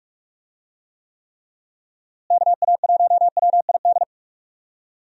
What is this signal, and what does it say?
Morse code, heard as a keyed tone, about 41 wpm, 700 Hz. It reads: KA1WID
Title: USA Call Signs - 6 Characters - Repeated 3x - 45wpm